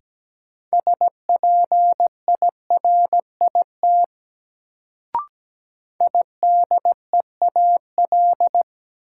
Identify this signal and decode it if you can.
Morse code, heard as a keyed tone, about 17 wpm, 700 Hz. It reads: SPIRIT IDEAL
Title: Sentences from Top 1400 Words - New Word Drill - 17wpm